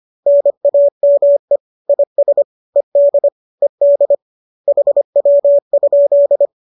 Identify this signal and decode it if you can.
Morse code, heard as a keyed tone, about 25 wpm, 570 Hz. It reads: NAME IS ED ED HW?
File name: qso205 25